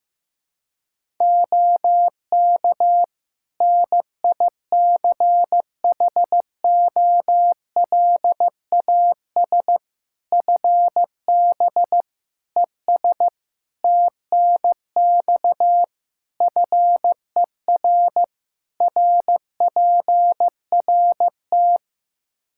Morse code, heard as a keyed tone, about 16 wpm, 700 Hz. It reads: OK NICHOLAS FB ES TNX FER RPRT
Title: Practice QSO - Exchange 2 - Info - 15wpm